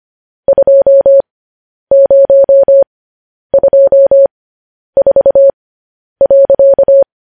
Morse code, heard as a keyed tone, about 25 wpm, spaced as for 10 wpm, 550 Hz. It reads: 2024.
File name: podcast / News headlines-1x-25x10